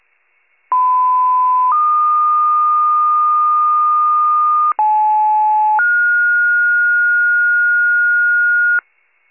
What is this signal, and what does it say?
Two-tone page, heard as a one-second tone, then a three-second tone, about 1010 then 1220 Hz, about 860 then 1430 Hz, with noise. It is heard from 0.7 to 8.8 s.
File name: pager / 75 FIRE / wcalert